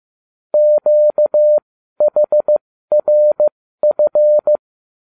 Morse code, heard as a keyed tone, about 15 wpm, spaced as for 12 wpm, 600 Hz. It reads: QHRF
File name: cw / ubacw31 1